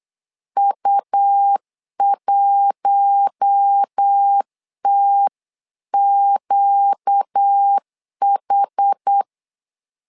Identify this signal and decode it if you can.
Morse code, heard as a keyed tone, about 8 wpm, 800 Hz. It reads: U1TQH
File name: LEZIONI / 5 LBVC / 5 LBVC 8wpm 2935